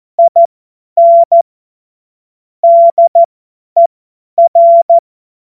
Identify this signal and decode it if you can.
Morse code, heard as a keyed tone, about 14 wpm, spaced as for 10 wpm, 680 Hz. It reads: IN DER
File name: Morsekurs / Witze / Witz 02. November 2021, 50 F70